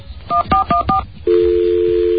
DTMF, heard as telephone touch tones, tones of about 120 ms. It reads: 1414